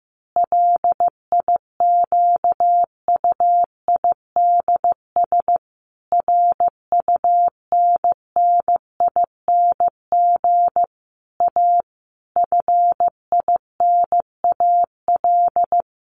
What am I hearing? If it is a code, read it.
Morse code, heard as a keyed tone, about 15 wpm, 700 Hz. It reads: LIQUIDS RUNNING A FINAL
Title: A Sound Of Thunder 15wpm